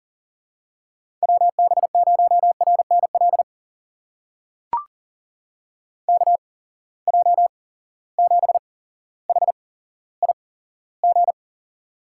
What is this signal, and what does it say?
Morse code, heard as a keyed tone, about 39 wpm, spaced as for 17 wpm, 700 Hz. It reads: WB0RNL KJ7HIG
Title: USA Call Signs - 6 Characters - 8x Character Spacing - 40wpm